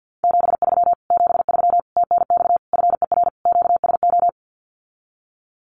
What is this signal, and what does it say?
Morse code, heard as a keyed tone, about 50 wpm, 700 Hz. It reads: 73 73 TNX FER QSO
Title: Common QSO Phrases Encore 50wpm